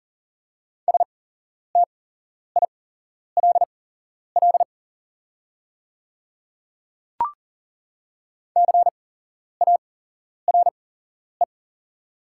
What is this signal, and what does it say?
Morse code, heard as a keyed tone, about 39 wpm, spaced as for 11 wpm, 700 Hz. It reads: STILL CARE
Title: 8x Character Spacing - Top 300 Words - 40wpm